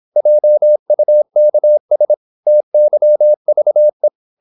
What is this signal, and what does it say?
Morse code, heard as a keyed tone, about 26 wpm, 600 Hz. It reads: JUKS TYVE
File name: Korte ord DK 26wpm